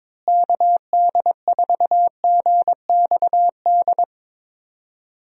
Morse code, heard as a keyed tone, about 22 wpm, 700 Hz. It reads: KD4GXD
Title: USA Call Signs - 6 Characters - Repeated 2x - 22wpm